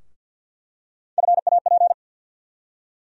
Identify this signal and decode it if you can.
Morse code, heard as a keyed tone, about 53 wpm, 700 Hz. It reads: FAP
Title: USA Call Sign Postfixes - 3 Letters 50wpm